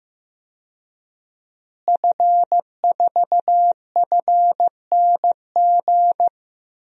Morse code, heard as a keyed tone, about 15 wpm, 700 Hz. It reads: F4FNG
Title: DXCC-100 Call Signs - 5 Characters - Repeated 2x - 15wpm